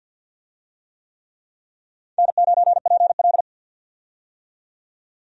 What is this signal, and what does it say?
Morse code, heard as a keyed tone, about 51 wpm, 700 Hz. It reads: N9PL
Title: USA Call Signs - 4 Characters - 50wpm